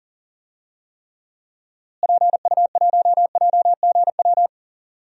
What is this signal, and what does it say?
Morse code, heard as a keyed tone, about 37 wpm, 700 Hz. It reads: PU1JGW
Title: DXCC-100 Call Signs - 6 Characters - Repeated 2x - 40wpm